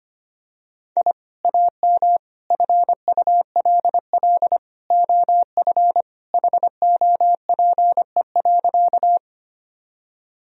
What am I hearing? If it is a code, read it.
Morse code, heard as a keyed tone, about 25 wpm, 700 Hz. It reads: I AM FULL OF HOPE.
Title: Sentences from Top 1000 Words - Review Part 3 - 25wpm